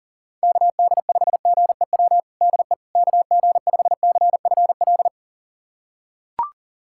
Morse code, heard as a keyed tone, about 40 wpm, 690 Hz. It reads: KD5GEW DE KG5CFL
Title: USA Callsign DE Callsign - 6 Characters - 40wpm